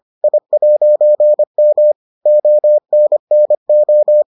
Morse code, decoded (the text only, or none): I'M ONNO